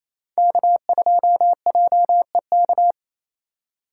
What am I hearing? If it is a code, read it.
Morse code, heard as a keyed tone, about 28 wpm, 700 Hz. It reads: K2JEK
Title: USA Call Signs - 4x Character Spacing - 28wpm